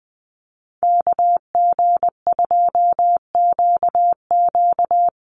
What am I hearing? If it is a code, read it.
Morse code, heard as a keyed tone, about 20 wpm, 700 Hz. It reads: KG2QQ
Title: USA Call Signs 20wpm